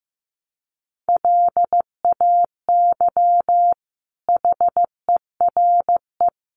Morse code, heard as a keyed tone, about 15 wpm, 700 Hz. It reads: LAY HERE
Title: Sentences from Top 1100 Words 15wpm